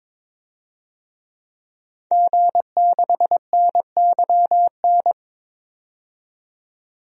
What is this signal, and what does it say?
Morse code, heard as a keyed tone, about 22 wpm, 700 Hz. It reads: G6NYN
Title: DXCC-100 Call Signs - 5 Characters - 22wpm